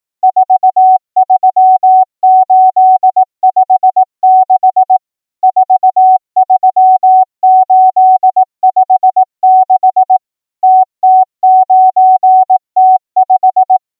Morse code, heard as a keyed tone, about 18 wpm, 750 Hz. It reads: 43856 43856 TT9T5
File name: numbers / V2-morse